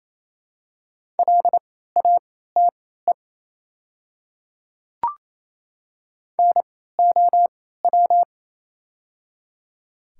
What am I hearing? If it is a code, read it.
Morse code, heard as a keyed tone, about 27 wpm, spaced as for 16 wpm, 700 Hz. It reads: LATE NOW
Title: 3x Character Spacing - Top 300 Words - 28wpm